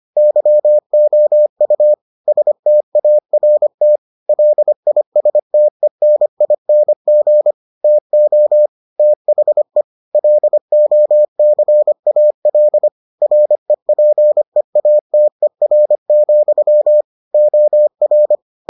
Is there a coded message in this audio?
Yes: Morse code from 0.2 to 18.4 s, heard as a keyed tone, about 25 wpm, 600 Hz. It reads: YOU START LISTENING TO THE LOCAL REPEATER, OR